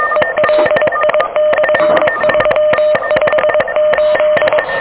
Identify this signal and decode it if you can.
Morse code, heard as a keyed tone, about 22 wpm, 610 Hz, with noise. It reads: ELI/358
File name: OK QRP 2023 / OK QRP 2023 OK1DWF